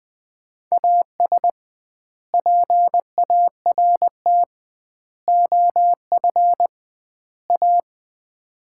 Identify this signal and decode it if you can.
Morse code, heard as a keyed tone, about 20 wpm, 700 Hz. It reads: AS PART OF A
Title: 2x Word-Spacing - Sets of 4 Words - 20wpm